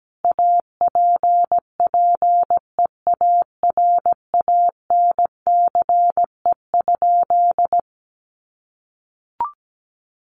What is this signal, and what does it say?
Morse code, heard as a keyed tone, about 17 wpm, 700 Hz. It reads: APPEARANCE?
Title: Sentences from Top 1100 Words 17wpm